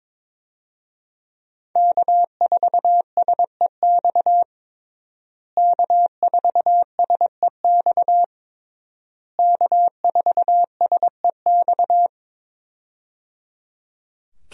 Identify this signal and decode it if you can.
Morse code, heard as a keyed tone, about 22 wpm, 700 Hz. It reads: K4SEX K4SEX K4SEX
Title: USA Call Signs - 5 Characters - Repeated 3x - 22wpm